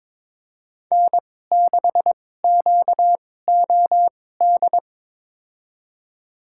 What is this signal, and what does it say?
Morse code, heard as a keyed tone, about 22 wpm, spaced as for 16 wpm, 700 Hz. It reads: N6QOD
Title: USA Call Signs - 5 Characters - 2x Character Spacing - 22wpm